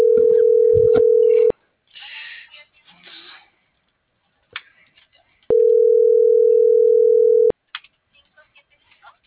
Ringback tone (North America), 440 + 480 Hz.